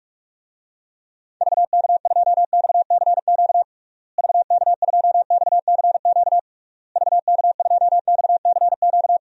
Morse code, heard as a keyed tone, about 46 wpm, 700 Hz. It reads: VK2XCQ VK2XCQ VK2XCQ
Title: DXCC-100 Call Signs - 6 Characters - Repeated 3x - 45wpm